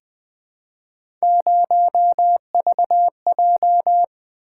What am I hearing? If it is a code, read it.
Morse code, heard as a keyed tone, about 20 wpm, 700 Hz. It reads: 0VJ
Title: Three Letter-Number Combinations - 20wpm